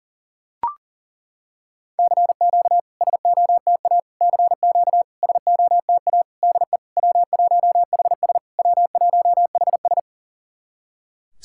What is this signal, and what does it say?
Morse code, heard as a keyed tone, about 40 wpm, 700 Hz. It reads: CQ SOTA CQ SOTA DE W1HS W1HS
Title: SOTA QSO Practice 40wpm